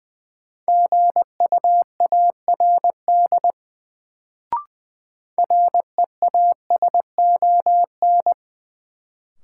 Morse code, heard as a keyed tone, about 20 wpm, 700 Hz. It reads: GUARD REASON
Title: Sentences from Top 700 Words - New Word Drill - 20wpm